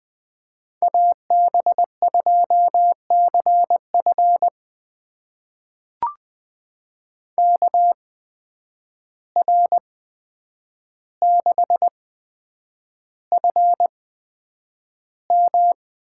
Morse code, heard as a keyed tone, about 20 wpm, spaced as for 9 wpm, 700 Hz. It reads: AB2CF KR6FM
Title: USA Call Signs - 5 Characters - 8x Character Spacing - 20wpm